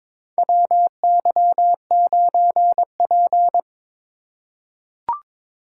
Morse code, heard as a keyed tone, about 22 wpm, 700 Hz. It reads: WY9P
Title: USA Call Signs - 4 Characters - 2x Character Spacing - 22wpm